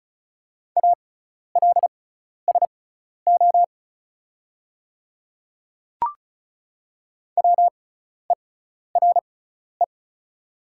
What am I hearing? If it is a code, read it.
Morse code, heard as a keyed tone, about 36 wpm, spaced as for 12 wpm, 700 Hz. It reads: ALSO WERE
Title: 6x Character Spacing - Top 200 Words - 35wpm